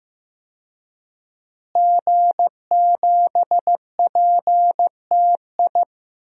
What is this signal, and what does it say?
Morse code, heard as a keyed tone, about 15 wpm, 700 Hz. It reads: G7PTI
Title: DXCC-100 Call Signs - 5 Characters - Repeated 3x - 15wpm